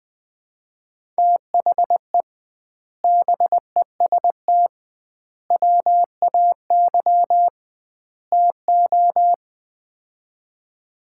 Morse code, heard as a keyed tone, about 21 wpm, 700 Hz. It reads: THE BEST WAY TO
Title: 2x Word-Spacing - Sets of 4 Words - 20wpm